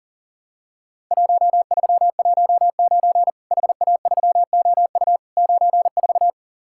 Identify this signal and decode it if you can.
Morse code, heard as a keyed tone, about 40 wpm, 700 Hz. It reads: 1319 HA3OU 94
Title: Worked All Europe DX Contest - QTC Exchange - 40wpm